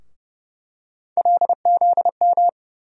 Morse code, heard as a keyed tone, about 31 wpm, 700 Hz. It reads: LZM
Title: USA Call Sign Postfixes - 3 Letters 30wpm